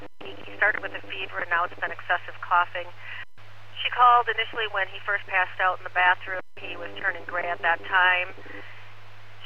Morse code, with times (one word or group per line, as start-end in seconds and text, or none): none